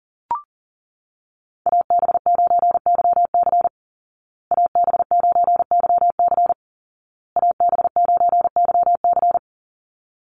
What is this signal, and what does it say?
Morse code, heard as a keyed tone, about 40 wpm, 700 Hz. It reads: AB9YC AB9YC AB9YC
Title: Call Signs Repeated 2x - 40wpm